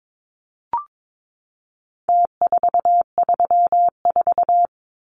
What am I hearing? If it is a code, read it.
Morse code, heard as a keyed tone, about 22 wpm, 700 Hz. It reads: T434
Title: Numbers - Contest Style - 4-Digit 22wpm